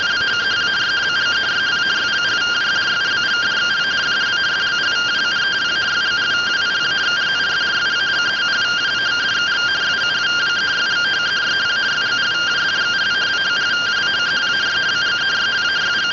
Two-tone teleprinter signal, about 170 Hz shift.